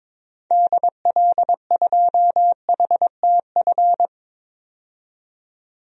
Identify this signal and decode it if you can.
Morse code, heard as a keyed tone, about 22 wpm, 700 Hz. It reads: DL2HTF
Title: DXCC-100 Call Signs - 6 Characters - 22wpm